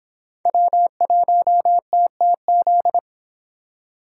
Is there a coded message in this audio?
Yes: Morse code from 0.5 to 3.0 s, heard as a keyed tone, about 25 wpm, 700 Hz. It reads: W1TTZ